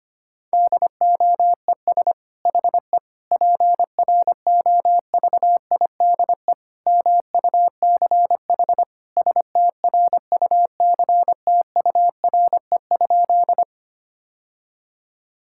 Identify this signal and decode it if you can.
Morse code, heard as a keyed tone, about 25 wpm, 700 Hz. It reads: DOES HE PROVIDE MUCH STRUCTURE?
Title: Sentences from Top 1000 Words - Review Part 6 - 25wpm